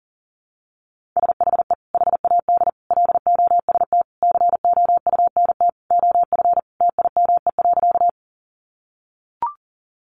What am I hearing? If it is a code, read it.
Morse code, heard as a keyed tone, about 40 wpm, 700 Hz. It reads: SHE HAD LOST COUNT OF TIME.
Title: Sentences from Top 1500 Words 40wpm